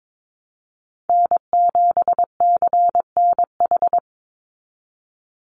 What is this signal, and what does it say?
Morse code, heard as a keyed tone, about 22 wpm, 700 Hz. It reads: N7CNH
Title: NA SOTA Chaser Callsigns 22wpm